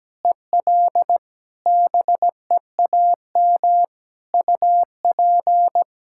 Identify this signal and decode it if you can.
Morse code, heard as a keyed tone, about 17 wpm, 700 Hz. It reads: EL BEAM UP
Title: Practice QSO - Exchange 2 - Info - 17wpm